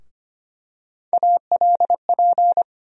Morse code, heard as a keyed tone, about 25 wpm, 700 Hz. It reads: ALP
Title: USA Call Sign Postfixes - 3 Letters 25wpm